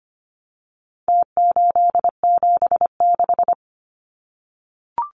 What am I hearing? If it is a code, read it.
Morse code, heard as a keyed tone, about 25 wpm, 700 Hz. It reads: T876
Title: Numbers - Contest Style - 4-Digit 25wpm